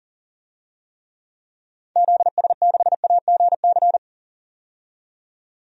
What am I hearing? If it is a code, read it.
Morse code, heard as a keyed tone, about 40 wpm, 700 Hz. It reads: ZS6AGC
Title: DXCC-100 Call Signs - 6 Characters - 40wpm